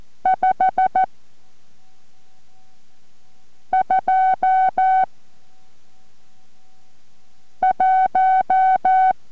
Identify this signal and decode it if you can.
Morse code, heard as a keyed tone, about 14 wpm, 750 Hz. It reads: 5 2 1